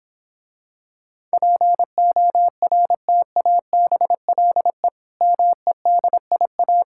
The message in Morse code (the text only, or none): PORTABLE MEDIA